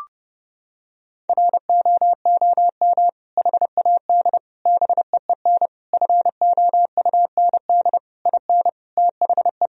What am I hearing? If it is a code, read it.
Morse code, heard as a keyed tone, about 30 wpm, 700 Hz. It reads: ROOM HAD BEEN FOUND IN THE
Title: Sentences from Top 1000 Words - Review Part 3 - 30wpm